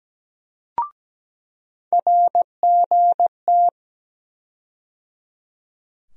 Morse code, heard as a keyed tone, about 17 wpm, 700 Hz. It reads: RGT
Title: USA Call Sign Postfixes - 3 Letters 17wpm